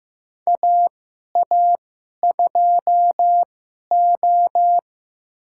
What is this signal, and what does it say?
Morse code, heard as a keyed tone, about 15 wpm, spaced as for 11 wpm, 700 Hz. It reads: AA2O
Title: USA Call Signs - 4 Characters - 2x Character Spacing - 15wpm